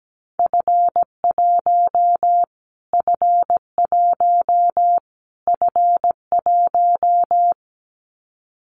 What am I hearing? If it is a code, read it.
Morse code, heard as a keyed tone, about 17 wpm, 700 Hz. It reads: F1 F1 F1
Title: Callsign Prefixes for Top 25 DXCC - Repeated 3x - 17wpm